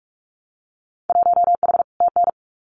Morse code, heard as a keyed tone, about 44 wpm, 700 Hz. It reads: 1H TN
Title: Winter Field Day Exchanges 45wpm